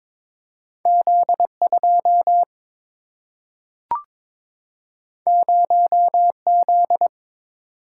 Morse code, heard as a keyed tone, about 22 wpm, 700 Hz. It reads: Z 2 0 Z
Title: Single-Letter Single-Number 22wpm